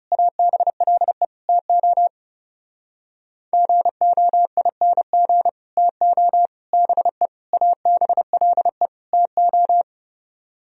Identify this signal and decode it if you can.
Morse code, heard as a keyed tone, about 40 wpm, 700 Hz. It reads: ABLE TO GOING TO BE ABLE TO